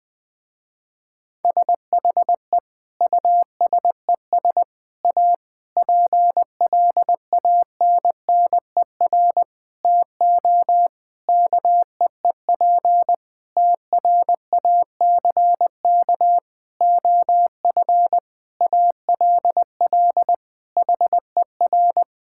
Morse code, heard as a keyed tone, about 21 wpm, 700 Hz. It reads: SHE USES A PLANNER TO KEEP TRACK OF ALL HER
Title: Difficult Sentences with Hints - 20wpm